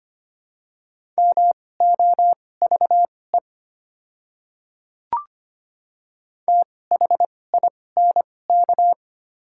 Morse code, heard as a keyed tone, about 25 wpm, spaced as for 18 wpm, 700 Hz. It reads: MOVE THINK
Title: 2x Character Spacing - Top 200 Words - 25wpm